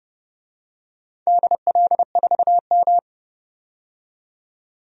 Morse code, heard as a keyed tone, about 29 wpm, 700 Hz. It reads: DL4M
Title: DXCC-100 Call Signs - 4 Characters - 30wpm